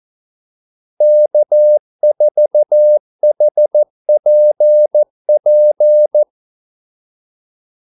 Morse code, heard as a keyed tone, about 14 wpm, 600 Hz. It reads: K4HPP
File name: Callsigns-14wpm